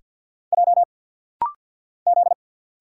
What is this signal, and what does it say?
Morse code, heard as a keyed tone, about 50 wpm, 700 Hz. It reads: J Z
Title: Single Letters - Rapid-Fire 50wpm